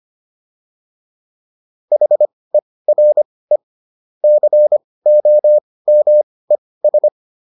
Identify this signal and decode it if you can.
Morse code, heard as a keyed tone, about 25 wpm, spaced as for 18 wpm, 600 Hz. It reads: HERE COMES